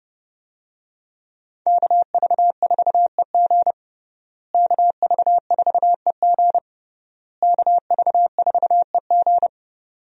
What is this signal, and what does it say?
Morse code, heard as a keyed tone, about 30 wpm, 700 Hz. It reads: KV4EG KV4EG KV4EG
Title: USA Call Signs - 5 Characters - Repeated 3x - 30wpm